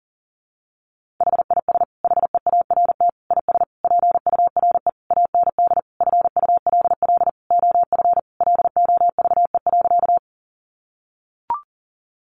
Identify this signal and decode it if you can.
Morse code, heard as a keyed tone, about 40 wpm, 700 Hz. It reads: HIS HEART IS PURE AND FULL OF LOVE.
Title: Sentences from Top 1500 Words 40wpm